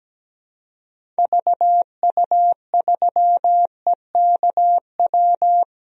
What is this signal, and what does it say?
Morse code, heard as a keyed tone, about 17 wpm, 700 Hz. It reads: VU3EKW
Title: DXCC-100 Call Signs - 6 Characters - Repeated 3x - 17wpm